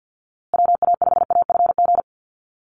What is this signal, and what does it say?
Morse code, heard as a keyed tone, about 48 wpm, 700 Hz. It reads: WA5AFG
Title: USA Call Signs Repeated 2x 50wpm